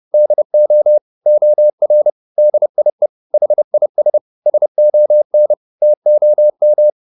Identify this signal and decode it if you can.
Morse code, heard as a keyed tone, about 30 wpm, 600 Hz. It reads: DO OR DIE HIS SON TOM